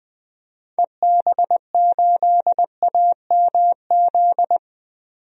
Morse code, heard as a keyed tone, about 20 wpm, 700 Hz. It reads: EB8AMZ
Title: DXCC-100 Call Signs - 6 Characters - 20wpm